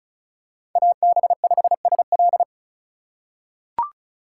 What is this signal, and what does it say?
Morse code, heard as a keyed tone, about 36 wpm, 700 Hz. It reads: AB5SL
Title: USA Call Signs - 5 Characters - 2x Character Spacing - 35wpm